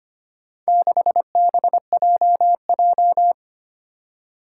Morse code, heard as a keyed tone, about 25 wpm, 700 Hz. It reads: 6BJJ
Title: Four Challenging Letter-Number Combinations 25wpm